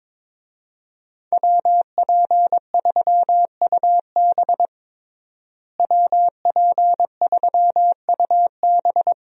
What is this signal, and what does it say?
Morse code, heard as a keyed tone, about 22 wpm, 700 Hz. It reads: WP3UB WP3UB
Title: USA Call Signs - 5 Characters - Repeated 2x - 22wpm